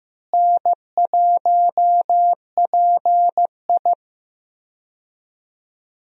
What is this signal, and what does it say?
Morse code, heard as a keyed tone, about 15 wpm, 700 Hz. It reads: N1PI
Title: USA Call Signs - 4 Characters - Repeated 2x - 15wpm